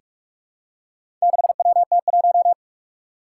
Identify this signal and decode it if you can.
Morse code, heard as a keyed tone, about 46 wpm, 700 Hz. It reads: 6WT1
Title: Four Challenging Letter-Number Combinations 45wpm